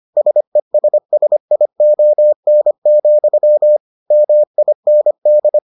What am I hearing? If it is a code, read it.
Morse code, heard as a keyed tone, about 25 wpm, 600 Hz. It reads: SESSION, MIND